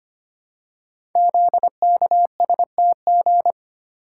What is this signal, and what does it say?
Morse code, heard as a keyed tone, about 25 wpm, 690 Hz. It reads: ZKSTG
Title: Code Groups - 5 Letters - 25wpm